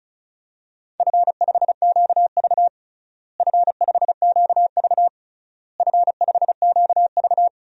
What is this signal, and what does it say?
Morse code, heard as a keyed tone, about 36 wpm, 700 Hz. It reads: F5QV F5QV F5QV
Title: DXCC-100 Call Signs - 4 Characters - Repeated 3x - 35wpm